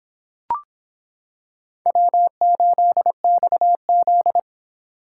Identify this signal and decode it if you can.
Morse code, heard as a keyed tone, about 25 wpm, 700 Hz. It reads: W8XZ